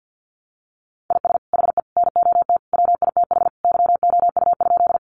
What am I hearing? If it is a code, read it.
Morse code, heard as a keyed tone, about 50 wpm, 700 Hz. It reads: IS HE NOT WITH YOU?
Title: Sentences from Top 600 Words 50wpm